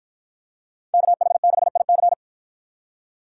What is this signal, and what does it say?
Morse code, heard as a keyed tone, about 53 wpm, 700 Hz. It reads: KH6IC